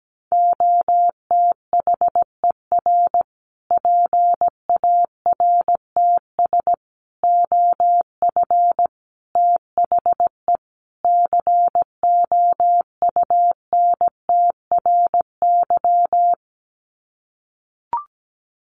Morse code, heard as a keyed tone, about 17 wpm, 700 Hz. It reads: OTHER PARTS OF THE COUNTRY
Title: Sets of 5 Words Encore 17wpm